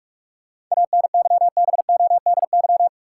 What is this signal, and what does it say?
Morse code, heard as a keyed tone, about 43 wpm, 700 Hz. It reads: ANYBODY
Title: Sets of 5 Words - New Word Drill - 45wpm